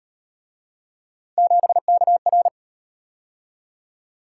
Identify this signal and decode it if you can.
Morse code, heard as a keyed tone, about 38 wpm, 700 Hz. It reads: 7KR